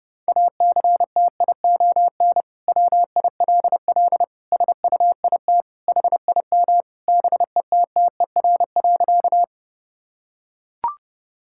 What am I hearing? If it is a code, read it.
Morse code, heard as a keyed tone, about 30 wpm, 700 Hz. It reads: ACTION WILL SUIT HIM BETTER.
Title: Sentences from Top 1600 Words 30wpm